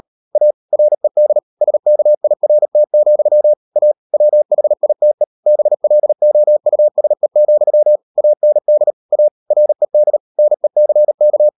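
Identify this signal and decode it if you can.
Morse code, heard as a keyed tone, about 38 wpm, 600 Hz. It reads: A RED SKIRT, A WHITE BLOUSE, AND A RED NECK
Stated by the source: Morse code practice at 38 WPM, sending Ken Follett's novel Never in English, from CW med Gnister.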